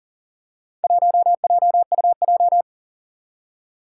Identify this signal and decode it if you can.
Morse code, heard as a keyed tone, about 44 wpm, 700 Hz. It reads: 1JUJ
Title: Four Challenging Letter-Number Combinations 40wpm